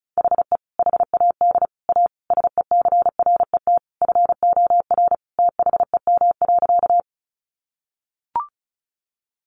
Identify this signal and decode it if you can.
Morse code, heard as a keyed tone, about 34 wpm, 700 Hz. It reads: HE HAD A SECRET FOR THEM.
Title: Sentences from Top 1200 Words 35wpm